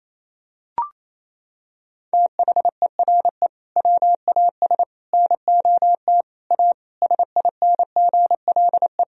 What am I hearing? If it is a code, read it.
Morse code, heard as a keyed tone, about 28 wpm, 700 Hz. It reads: THERE WAS NOT A SINGLE
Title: Sentences from Top 1000 Words - Review Part 4 - 28wpm